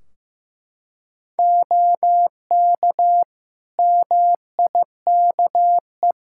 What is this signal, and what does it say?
Morse code, heard as a keyed tone, about 15 wpm, 700 Hz. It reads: OK MIKE
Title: Practice QSO - Exchange 3 - More Info - 15wpm